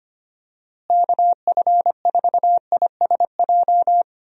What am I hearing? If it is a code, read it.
Morse code, heard as a keyed tone, about 25 wpm, 700 Hz. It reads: KF4ISJ
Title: USA Call Signs - 6 Characters - Repeated 3x - 25wpm